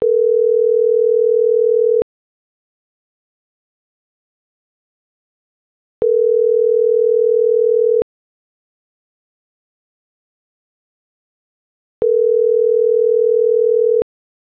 Ringback tone (North America), 440 + 480 Hz.